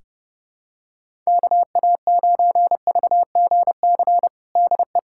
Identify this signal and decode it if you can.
Morse code, heard as a keyed tone, about 29 wpm, 700 Hz. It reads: KA9VGC DE
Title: USA Callsign DE Callsign - 6 Characters - 30wpm